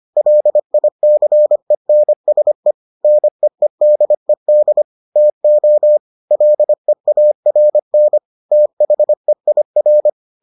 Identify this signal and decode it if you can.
Morse code, heard as a keyed tone, about 25 wpm, 600 Hz. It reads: LICENSE NEEDED TO LEARN THEIR